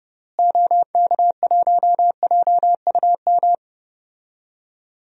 Morse code, heard as a keyed tone, about 32 wpm, 700 Hz. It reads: OK1JUM
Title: DXCC-100 Call Signs - 6 Characters - Repeated 2x - 30wpm